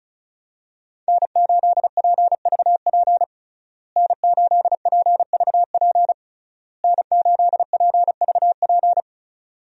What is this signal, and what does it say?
Morse code, heard as a keyed tone, about 36 wpm, 700 Hz. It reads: N8PVP N8PVP N8PVP
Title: USA Call Signs - 5 Characters - Repeated 3x - 35wpm